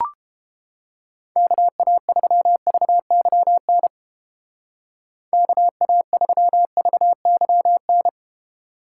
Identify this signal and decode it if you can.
Morse code, heard as a keyed tone, about 29 wpm, 700 Hz. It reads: KA3VYN KA3VYN